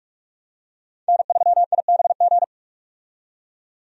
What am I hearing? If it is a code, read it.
Morse code, heard as a keyed tone, about 42 wpm, 700 Hz. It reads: N3IBG
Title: USA Call Signs - 5 Characters - 45wpm